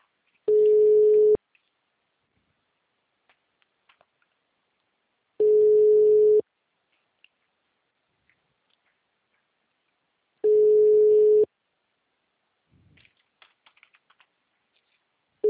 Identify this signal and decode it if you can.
Ringback tone (Europe, 425 Hz).